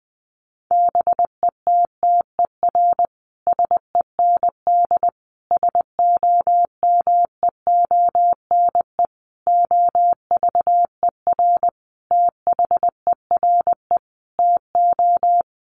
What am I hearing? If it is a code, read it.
Morse code, heard as a keyed tone, about 20 wpm, 700 Hz. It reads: BETTER SEND SOMEONE OVER THERE TO
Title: Sentences from Top 1500 Words 20wpm